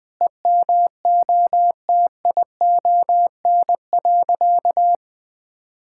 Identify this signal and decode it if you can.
Morse code, heard as a keyed tone, about 20 wpm, 700 Hz. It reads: EMOTION.
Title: Sentences from Top 1600 Words 20wpm